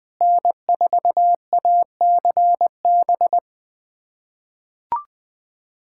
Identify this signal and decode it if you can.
Morse code, heard as a keyed tone, about 20 wpm, 690 Hz. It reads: N4ACB
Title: USA Callsign DE Callsign 20wpm